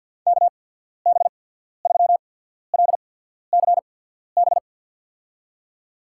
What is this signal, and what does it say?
Morse code, heard as a keyed tone, about 53 wpm, spaced as for 14 wpm, 700 Hz. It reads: KB3LCB
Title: USA Call Signs - 6 Characters - 8x Character Spacing - 50wpm